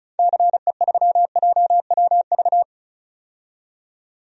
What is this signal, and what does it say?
Morse code, heard as a keyed tone, about 36 wpm, 700 Hz. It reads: CE3JWV
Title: DXCC-100 Call Signs - 6 Characters - 35wpm